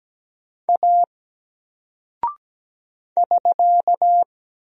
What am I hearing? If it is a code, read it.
Morse code, heard as a keyed tone, about 17 wpm, 700 Hz. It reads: A <SK>
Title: Single Character (Letter:Number:Punctuation:Prosign) 17wpm